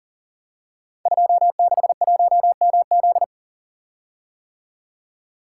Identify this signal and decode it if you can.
Morse code, heard as a keyed tone, about 39 wpm, 700 Hz. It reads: 261MZ
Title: Code Groups - 5 Letter:Numbers - 40wpm